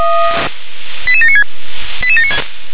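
Five-tone selective call (ZVEI), heard as quick runs of five tones, tones of about 70 ms. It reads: 88796 88796
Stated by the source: Five-tone ZVEI selective call 88792 received by SDR.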